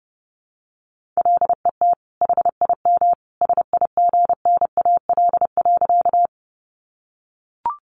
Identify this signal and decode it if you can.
Morse code, heard as a keyed tone, about 30 wpm, 700 Hz. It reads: LET HIM SIGNAL.
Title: Sentences from Top 1400 Words 30wpm